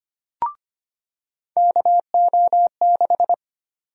Morse code, heard as a keyed tone, about 25 wpm, 700 Hz. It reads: KO6
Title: USA Call Sign Prefixes - 25wpm